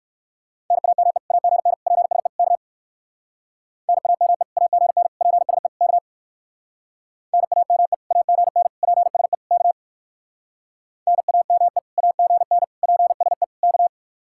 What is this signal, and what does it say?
Morse code, heard as a keyed tone, about 58 wpm, 700 Hz. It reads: NAME AGN PSE K NAME AGN PSE K NAME AGN PSE K NAME AGN PSE K